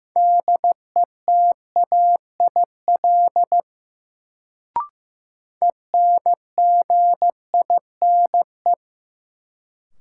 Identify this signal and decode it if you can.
Morse code, heard as a keyed tone, about 15 wpm, 700 Hz. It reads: DETAIL ENGINE
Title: Sentences from Top 1500 Words - New Word Drill - 15wpm